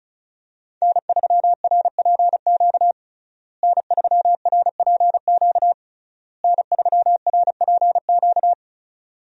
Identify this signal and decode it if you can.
Morse code, heard as a keyed tone, about 35 wpm, 700 Hz. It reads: N3RPQ N3RPQ N3RPQ
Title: USA Call Signs - 5 Characters - Repeated 3x - 35wpm